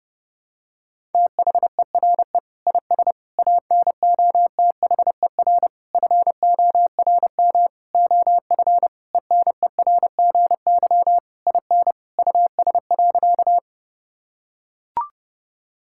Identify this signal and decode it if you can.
Morse code, heard as a keyed tone, about 30 wpm, 700 Hz. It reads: THERE IS ANOTHER FORM OF ENERGY IN US.
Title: Sentences from Top 1000 Words - Review Part 6 - 30wpm